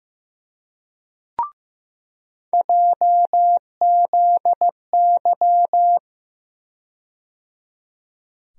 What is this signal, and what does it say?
Morse code, heard as a keyed tone, about 15 wpm, 700 Hz. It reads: JZY
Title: USA Call Sign Postfixes - 3 Letters 15wpm